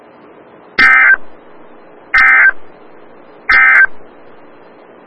Emergency Alert System (SAME) tones, heard as harsh buzzing bursts. It is heard from 0.7 to 3.9 s.
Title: Des Moines, IA NOAA Weather Radio - WXL57